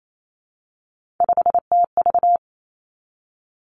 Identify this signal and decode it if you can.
Morse code, heard as a keyed tone, about 28 wpm, 700 Hz. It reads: 5TV
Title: Three Challenging Letter-Number Combinations 28wpm